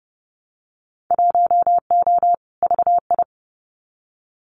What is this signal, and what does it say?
Morse code, heard as a keyed tone, about 30 wpm, 700 Hz. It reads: 1O VI
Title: Winter Field Day Exchanges Encore 30wpm